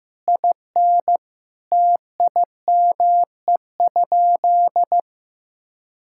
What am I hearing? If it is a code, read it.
Morse code, heard as a keyed tone, about 16 wpm, 700 Hz. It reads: IN TIME?
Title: Sentences from Top 1000 Words - Review Part 5 - 15wpm